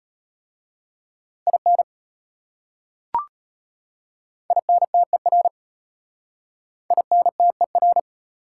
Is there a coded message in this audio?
Yes: Morse code from 1.5 to 8.0 s, heard as a keyed tone, about 33 wpm, 700 Hz. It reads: IN INTER INTER